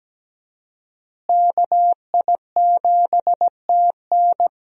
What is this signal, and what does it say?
Morse code, heard as a keyed tone, about 17 wpm, 700 Hz. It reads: KI7TN
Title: USA Call Signs - 5 Characters - Repeated 2x - 17wpm